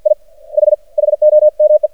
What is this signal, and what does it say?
Morse code, heard as a keyed tone, about 51 wpm, 610 Hz. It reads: I H HOME
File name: archived / CRSnet